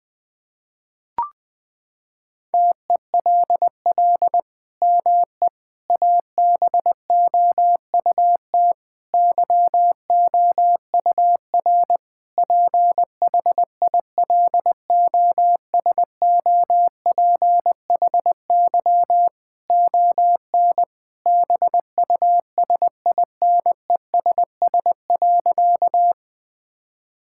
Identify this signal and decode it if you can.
Morse code, heard as a keyed tone, about 21 wpm, 700 Hz. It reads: TELL ME ABOUT YOUR PHILOSOPHY ON BUSINESS.
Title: Sentences from Top 1000 Words - Review Part 6 - 20wpm